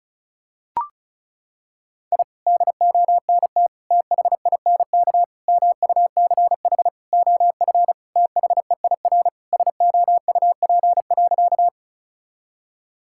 Morse code, heard as a keyed tone, about 36 wpm, 700 Hz. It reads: I DONT THINK MUCH OF THEIR SOUP.
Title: Sentences from Top 1000 Words - Review Part 6 - 35wpm